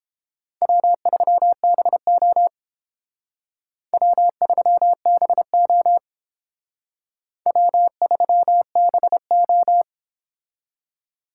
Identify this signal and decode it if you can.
Morse code, heard as a keyed tone, about 28 wpm, 700 Hz. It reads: W3BO W3BO W3BO